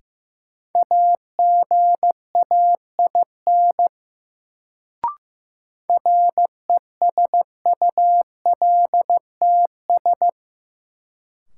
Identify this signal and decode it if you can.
Morse code, heard as a keyed tone, about 15 wpm, 700 Hz. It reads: AGAIN RESULTS
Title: Sets of 4 Words - New Word Drill - 15wpm